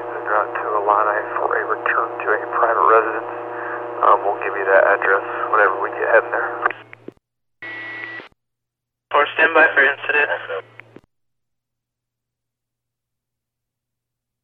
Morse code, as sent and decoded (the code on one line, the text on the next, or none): none